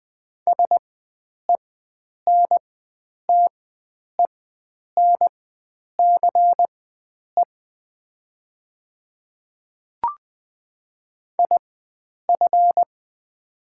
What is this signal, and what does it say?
Morse code, heard as a keyed tone, about 20 wpm, spaced as for 9 wpm, 690 Hz. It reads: SENTENCE IF
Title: 4x Character Spacing - Top 300 Words - 20wpm